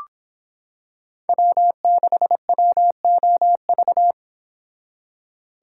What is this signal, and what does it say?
Morse code, heard as a keyed tone, about 26 wpm, 700 Hz. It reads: W6WOV